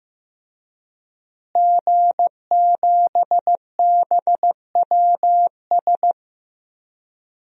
Morse code, heard as a keyed tone, about 15 wpm, 700 Hz. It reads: G7BWS
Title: DXCC-100 Call Signs - 5 Characters - Repeated 3x - 15wpm